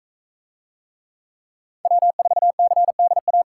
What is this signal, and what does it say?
Morse code, heard as a keyed tone, about 41 wpm, 700 Hz. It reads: W4CDA